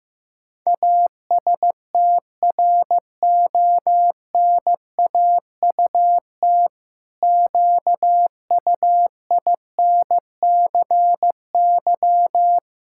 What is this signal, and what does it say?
Morse code, heard as a keyed tone, about 16 wpm, 700 Hz. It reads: ASTRONAUT QUINCY
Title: Pangram Challenge v2 - 15wpm